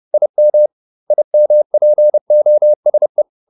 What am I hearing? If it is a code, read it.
Morse code, heard as a keyed tone, about 30 wpm, 600 Hz. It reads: IM IMPOSE
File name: IM6 30